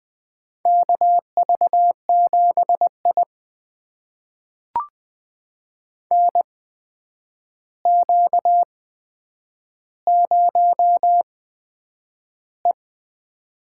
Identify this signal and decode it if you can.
Morse code, heard as a keyed tone, about 20 wpm, spaced as for 9 wpm, 700 Hz. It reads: KV7I NQ0E
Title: USA Call Signs - 4 Characters - 8x Character Spacing - 20wpm